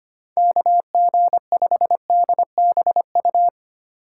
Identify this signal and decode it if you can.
Morse code, heard as a keyed tone, about 25 wpm, 700 Hz. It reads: KG5DBU
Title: USA Call Signs - 6 Characters - 4x Character Spacing - 25wpm